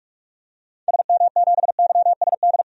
Morse code, heard as a keyed tone, about 44 wpm, 700 Hz. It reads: SM7YSD
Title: DXCC-100 Call Signs - 6 Characters - 45wpm